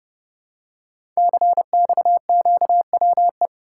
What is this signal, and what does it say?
Morse code, heard as a keyed tone, about 29 wpm, 700 Hz. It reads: CXQWE
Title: Code Groups - 5 Letters - 30wpm